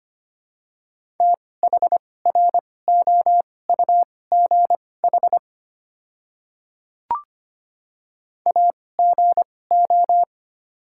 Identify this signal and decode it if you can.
Morse code, heard as a keyed tone, about 25 wpm, spaced as for 18 wpm, 700 Hz. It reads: THROUGH AGO
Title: 2x Character Spacing - Top 500 Words - 25wpm